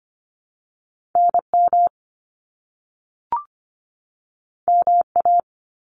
Morse code, heard as a keyed tone, about 25 wpm, 700 Hz. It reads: NM MA
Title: US States 25wpm